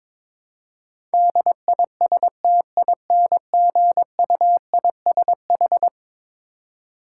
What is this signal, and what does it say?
Morse code, heard as a keyed tone, about 21 wpm, 700 Hz. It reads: DISTINGUISH
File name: practice / 22wpm / Eleven Letter Words - 22wpm